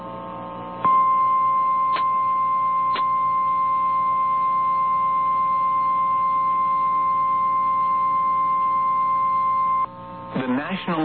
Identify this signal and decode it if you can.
Emergency Alert System (SAME) tones (the 1050 Hz attention tone), with speech. It is heard from 0.8 to 9.9 s.
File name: eas / KNCYKBIE / KNCYKBIE